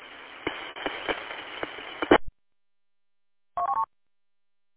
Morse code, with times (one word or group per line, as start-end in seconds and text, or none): none